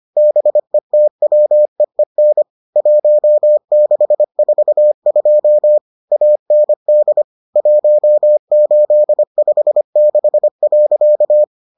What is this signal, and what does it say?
Morse code, heard as a keyed tone, about 25 wpm, 600 Hz. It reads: BETWEEN 1642 AND 1856.